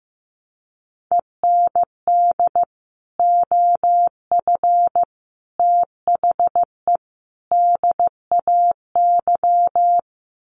Morse code, heard as a keyed tone, about 15 wpm, 700 Hz. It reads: END OF THE DAY
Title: Sets of 4 Words 15wpm